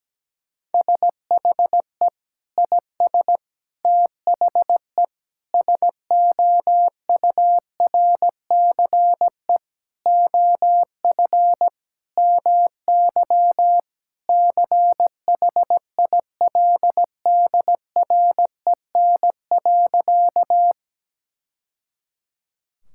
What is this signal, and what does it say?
Morse code, heard as a keyed tone, about 17 wpm, 700 Hz. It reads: SHE IS THE SOURCE OF MY CHILDREN.
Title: Sentences from Top 1000 Words - Review Part 5 - 17wpm